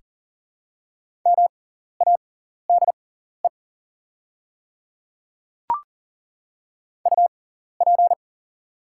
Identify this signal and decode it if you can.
Morse code, heard as a keyed tone, about 40 wpm, spaced as for 14 wpm, 700 Hz. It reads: MADE UP
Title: 6x Character Spacing - Top 500 Words - 40wpm